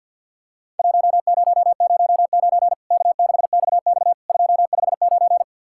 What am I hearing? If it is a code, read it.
Morse code, heard as a keyed tone, about 48 wpm, 700 Hz. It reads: 1009 K6XX 259
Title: Worked All Europe DX Contest - QTC Exchange - 50wpm